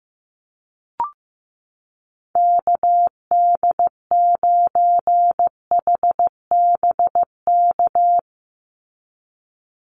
Morse code, heard as a keyed tone, about 15 wpm, 700 Hz. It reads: KD9HBK
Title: USA Call Signs 15wpm